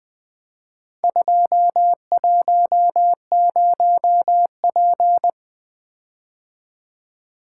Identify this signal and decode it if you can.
Morse code, heard as a keyed tone, about 22 wpm, 700 Hz. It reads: 210P
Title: Speed-Racing - Four Letter-Number Combinations - 20wpm